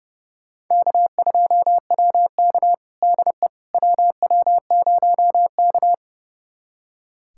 Morse code, heard as a keyed tone, about 31 wpm, 700 Hz. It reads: K2WK DE WW0K
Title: USA Callsign DE Callsign - 4 Characters - 30wpm